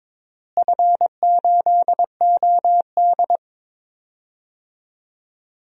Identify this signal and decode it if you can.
Morse code, heard as a keyed tone, about 22 wpm, 700 Hz. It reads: F8OD
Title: DXCC-100 Call Signs - 4 Characters - Repeated 2x - 22wpm